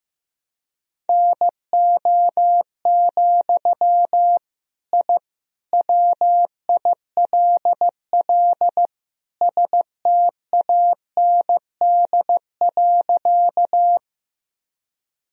Morse code, heard as a keyed tone, about 15 wpm, 700 Hz. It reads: NO, I WILL STAND.
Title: Sentences from Top 1000 Words - Review Part 1 - 15wpm